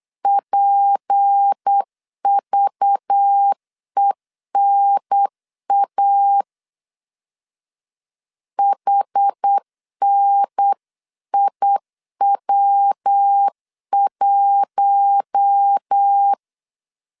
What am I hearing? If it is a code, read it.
Morse code, heard as a keyed tone, about 8 wpm, 800 Hz. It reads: PVENA HNIW1
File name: CW / DET / 1 6 1500C S B 8wpm 5142